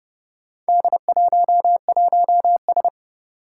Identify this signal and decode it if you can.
Morse code, heard as a keyed tone, about 30 wpm, 700 Hz. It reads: D11S
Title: Four Challenging Letter-Number Combinations 30wpm